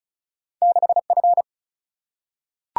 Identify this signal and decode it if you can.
Morse code, heard as a keyed tone, about 36 wpm, 700 Hz. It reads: 6 F
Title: Single-Letter Single-Number 35wpm